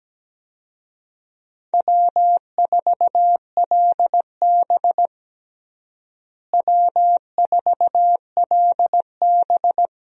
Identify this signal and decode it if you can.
Morse code, heard as a keyed tone, about 17 wpm, 700 Hz. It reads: W4LB W4LB
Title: USA Call Signs - 4 Characters - Repeated 2x - 17wpm